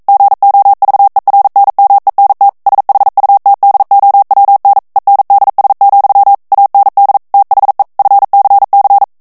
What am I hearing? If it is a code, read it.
Morse code, heard as a keyed tone, about 42 wpm, 800 Hz. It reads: GOVERNMENT SHUTDOWN ENDS, AND THE FCC
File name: 40 mots minute